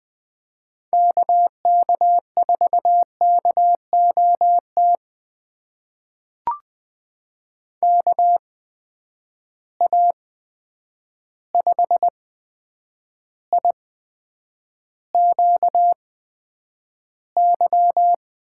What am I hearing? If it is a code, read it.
Morse code, heard as a keyed tone, about 20 wpm, spaced as for 9 wpm, 700 Hz. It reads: KK4KOT KA5IQY
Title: USA Call Signs - 6 Characters - 8x Character Spacing - 20wpm